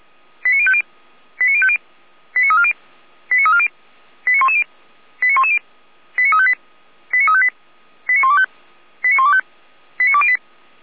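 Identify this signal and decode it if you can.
Five-tone selective call (ZVEI), heard as quick runs of five tones, tones of about 70 ms. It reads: 89050 89050 89340 89340 89110 89110 89358 89358 89125 89125 89208
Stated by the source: ZVEI five-tone selective call received on SDR.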